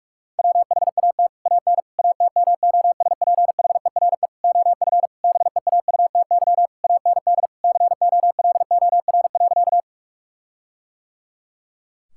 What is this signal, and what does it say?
Morse code, heard as a keyed tone, about 46 wpm, 700 Hz. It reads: WHAT AN ATMOSPHERE OF BEAUTY AND COLOR.
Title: Sentences from Top 1600 Words 45wpm